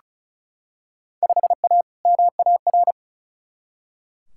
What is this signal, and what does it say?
Morse code, heard as a keyed tone, about 36 wpm, 700 Hz. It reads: 5A MAR
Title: Field Day Exchanges Encore - 35wpm